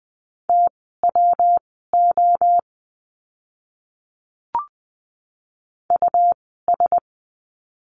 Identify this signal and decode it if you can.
Morse code, heard as a keyed tone, about 20 wpm, spaced as for 15 wpm, 700 Hz. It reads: TWO US
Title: 146 Lesson 24f - Letter y - Words 20wpm